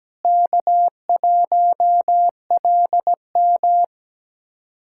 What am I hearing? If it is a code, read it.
Morse code, heard as a keyed tone, about 17 wpm, 700 Hz. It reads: K1LM
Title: USA Callsign DE Callsign - 4 Characters - 17wpm